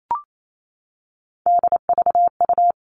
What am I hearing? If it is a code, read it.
Morse code, heard as a keyed tone, about 28 wpm, 700 Hz. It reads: DVU
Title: All Cut Numbers - 3-Digit 28wpm